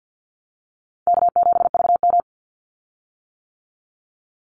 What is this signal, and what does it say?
Morse code, heard as a keyed tone, about 50 wpm, 700 Hz. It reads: K7VM
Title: Easy Call Signs - 50wpm